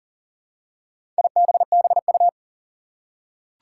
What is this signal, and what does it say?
Morse code, heard as a keyed tone, about 40 wpm, 700 Hz. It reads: IBBU
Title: Four Challenging Letter-Number Combinations 40wpm